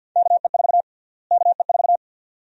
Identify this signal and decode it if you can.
Morse code, heard as a keyed tone, about 53 wpm, 700 Hz. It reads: KE4 KE4
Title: USA Call Sign Prefixes - Repeated 2x - 50wpm